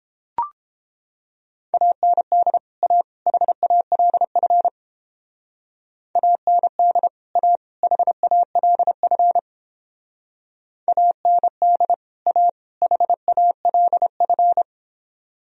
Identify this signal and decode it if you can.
Morse code, heard as a keyed tone, about 29 wpm, 700 Hz. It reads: AND A HALF AND A HALF AND A HALF